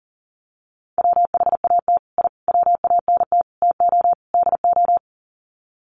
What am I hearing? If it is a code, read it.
Morse code, heard as a keyed tone, about 40 wpm, 700 Hz. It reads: WHAT I WANT TO DO
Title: Sets of 5 Words Encore 40wpm